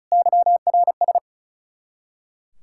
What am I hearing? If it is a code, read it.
Morse code, heard as a keyed tone, about 46 wpm, 700 Hz. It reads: YRS